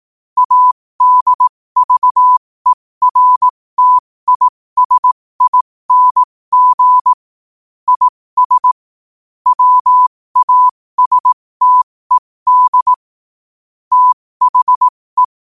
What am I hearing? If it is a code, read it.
Morse code, heard as a keyed tone, about 18 wpm, spaced as for 15 wpm, 1000 Hz. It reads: ADVERTISING IS WASTED THE
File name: qotdfiles / 15wpm